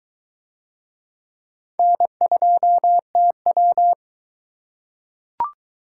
Morse code, heard as a keyed tone, about 22 wpm, 700 Hz. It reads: N2TW